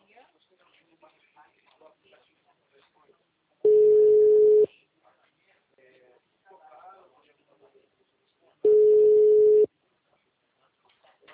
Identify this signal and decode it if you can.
Ringback tone (Europe, 425 Hz), with speech.